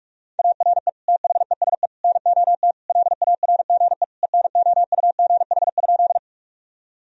Morse code, heard as a keyed tone, about 46 wpm, 700 Hz. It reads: ARE THESE NOT LARGE ENOUGH?
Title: Sentences from Top 1000 Words - Review Part 1 - 45wpm